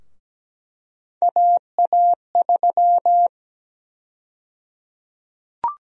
Morse code, heard as a keyed tone, about 17 wpm, 700 Hz. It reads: AA3
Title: USA Call Sign Prefixes - Repeated 2x - 17wpm